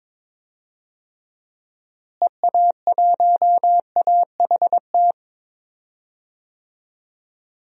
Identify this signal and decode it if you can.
Morse code, heard as a keyed tone, about 22 wpm, 700 Hz. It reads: EA1AHT
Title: DXCC-100 Call Signs - 6 Characters - Repeated 2x - 22wpm